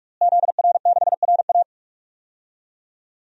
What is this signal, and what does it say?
Morse code, heard as a keyed tone, about 46 wpm, 700 Hz. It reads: ZR6RA
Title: DXCC-100 Call Signs - 5 Characters - Repeated 3x - 45wpm